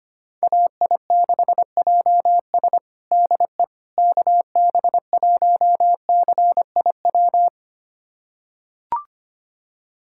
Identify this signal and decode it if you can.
Morse code, heard as a keyed tone, about 25 wpm, 700 Hz. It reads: AI6JS DE KB1CIW
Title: USA Callsign DE Callsign 25wpm